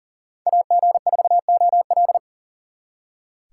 Morse code, heard as a keyed tone, about 52 wpm, 700 Hz. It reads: AG4OL